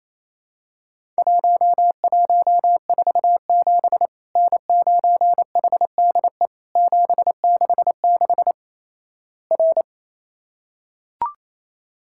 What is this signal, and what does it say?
Morse code, heard as a keyed tone, about 28 wpm, 700 Hz. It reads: 1147 N9HDE 766 R
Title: Worked All Europe DX Contest - QTC Exchange - 28wpm